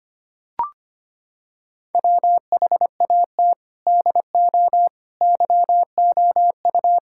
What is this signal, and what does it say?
Morse code, heard as a keyed tone, about 25 wpm, 700 Hz. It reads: WHAT DO YOU
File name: practice / 25wpm / Sentences with 3-Word Phrases - 25wpm